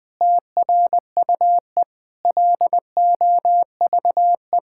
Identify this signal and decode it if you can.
Morse code, heard as a keyed tone, about 20 wpm, 700 Hz. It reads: TRUE LOVE
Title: Sets of 9 Words - 20wpm